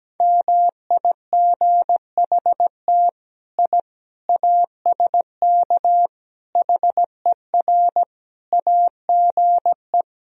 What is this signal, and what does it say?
Morse code, heard as a keyed tone, about 17 wpm, 700 Hz. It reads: MIGHT I ASK HER AGE
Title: Sentences from Top 1000 Words - Review Part 3 - 17wpm